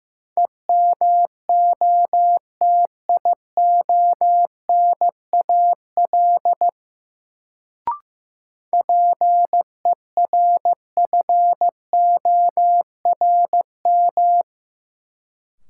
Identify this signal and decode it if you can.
Morse code, heard as a keyed tone, about 15 wpm, 700 Hz. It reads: EMOTIONAL PERFORM
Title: Sentences from Top 1000 Words - Review Part 5 - New Word Drill - 15wpm